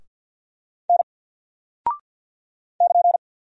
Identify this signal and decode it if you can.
Morse code, heard as a keyed tone, about 48 wpm, 700 Hz. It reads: N <KN>
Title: Single Character (Letter:Number:Punctuation:Prosign) 50wpm